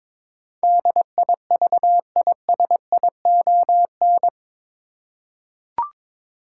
Morse code, heard as a keyed tone, about 22 wpm, 700 Hz. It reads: DIVISION
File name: practice / 22wpm / Eight Letter Words - 22wpm